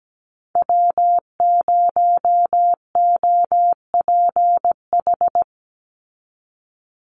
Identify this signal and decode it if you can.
Morse code, heard as a keyed tone, about 17 wpm, 700 Hz. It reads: W0OPH
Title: USA Call Signs 17wpm